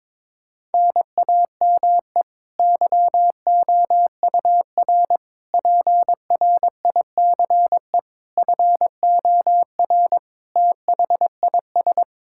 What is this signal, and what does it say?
Morse code, heard as a keyed tone, about 22 wpm, 700 Hz. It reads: NAME YOUR PRICE FOR THIS
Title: Sentences from Top 1000 Words - Review Part 5 - 22wpm